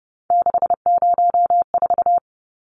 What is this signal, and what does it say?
Morse code, heard as a keyed tone, about 30 wpm, 700 Hz. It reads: 604
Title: Numbers - 3-Digit 30wpm